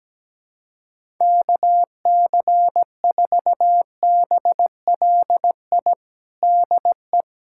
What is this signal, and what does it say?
Morse code, heard as a keyed tone, about 17 wpm, 700 Hz. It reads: KC4BLI DE
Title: USA Callsign DE Callsign 17wpm